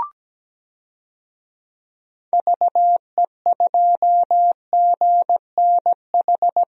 Morse code, heard as a keyed tone, about 17 wpm, 700 Hz. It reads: VE2GNH
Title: DXCC-100 Call Signs - 6 Characters - Repeated 3x - 17wpm